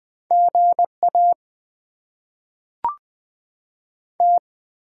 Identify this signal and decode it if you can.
Morse code, heard as a keyed tone, about 20 wpm, 690 Hz. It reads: GA T
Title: Top 20 QSO Elements - 6x Character Spacing - 20wpm